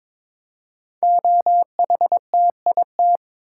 Morse code, heard as a keyed tone, about 22 wpm, 700 Hz. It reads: OHTIT
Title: Code Groups - 5 Letters - 22wpm